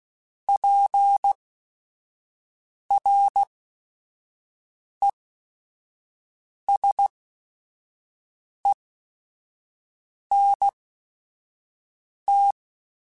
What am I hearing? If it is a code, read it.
Morse code, heard as a keyed tone, about 16 wpm, spaced as for 5 wpm, 800 Hz. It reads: PRESENT
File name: Technical Subjects